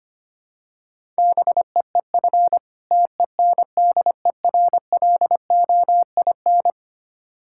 Morse code, heard as a keyed tone, about 25 wpm, 700 Hz. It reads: BEEF TENDERLOIN
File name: practice / 25wpm / Difficult Words with Hints - 25wpm